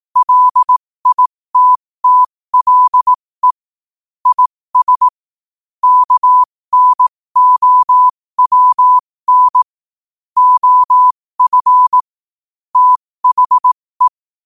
Morse code, heard as a keyed tone, about 18 wpm, spaced as for 15 wpm, 1000 Hz. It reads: LITTLE IS KNOWN OF THE
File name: qotdfiles / 15wpm